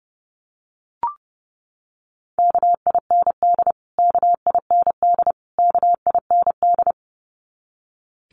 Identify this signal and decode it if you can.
Morse code, heard as a keyed tone, about 30 wpm, 700 Hz. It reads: KIND KIND KIND
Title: Top 500 Words Repeated 30wpm